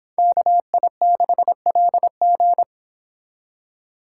Morse code, heard as a keyed tone, about 26 wpm, 700 Hz. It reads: KI6LG